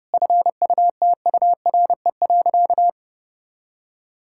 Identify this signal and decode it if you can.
Morse code, heard as a keyed tone, about 32 wpm, 700 Hz. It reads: FUTURE.
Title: Sentences from Top 1000 Words - Review Part 6 - 30wpm